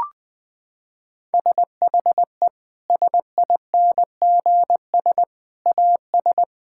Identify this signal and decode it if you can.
Morse code, heard as a keyed tone, about 20 wpm, 700 Hz. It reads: SHE SINGS AS
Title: Sentences with 3-Word Phrases - 20wpm